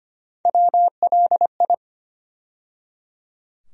Morse code, heard as a keyed tone, about 25 wpm, 700 Hz. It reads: WLI
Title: USA Call Sign Postfixes - 3 Letters 25wpm